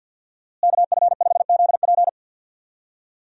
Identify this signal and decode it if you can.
Morse code, heard as a keyed tone, about 50 wpm, 700 Hz. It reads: KF5ZP
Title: USA Call Signs - 5 Characters - Repeated 3x - 50wpm